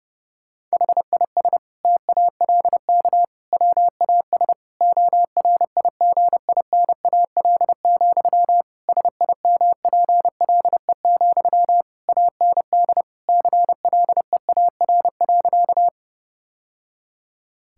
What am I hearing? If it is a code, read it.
Morse code, heard as a keyed tone, about 30 wpm, 700 Hz. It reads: HIS TALK WAS ORIGINAL, SIMPLE, AND CLEAR.
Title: Sentences from Top 1000 Words - Review Part 6 - 30wpm